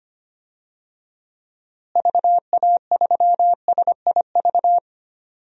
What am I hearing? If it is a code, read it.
Morse code, heard as a keyed tone, about 25 wpm, 700 Hz. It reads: VA3SIV
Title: DXCC-100 Call Signs - 6 Characters - Repeated 2x - 25wpm